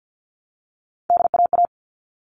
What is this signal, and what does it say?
Morse code, heard as a keyed tone, about 50 wpm, 700 Hz. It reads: DAA
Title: All Cut Numbers - 3-Digit 50wpm